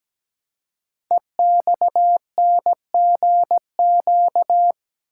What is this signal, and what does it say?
Morse code, heard as a keyed tone, about 17 wpm, 700 Hz. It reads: EXNGQ
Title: Code Groups - 5 Letters - 17wpm